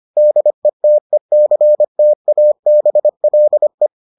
Morse code, heard as a keyed tone, about 25 wpm, 600 Hz. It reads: DETECTABLE